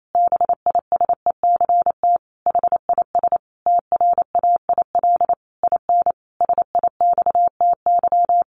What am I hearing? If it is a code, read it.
Morse code, heard as a keyed tone, about 28 wpm, 700 Hz. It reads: BISECT HIS TRAIL IN SIXTY
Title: A Sound Of Thunder 28wpm